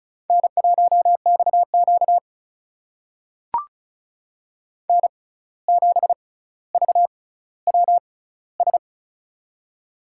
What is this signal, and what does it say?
Morse code, heard as a keyed tone, about 36 wpm, spaced as for 17 wpm, 700 Hz. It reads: N1XQ N7VWS
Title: USA Call Signs - 6x Character Spacing - 35wpm